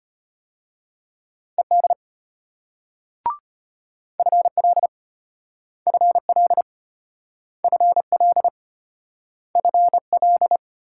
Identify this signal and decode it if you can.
Morse code, heard as a keyed tone, about 32 wpm, 700 Hz. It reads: ED FL FL FL FL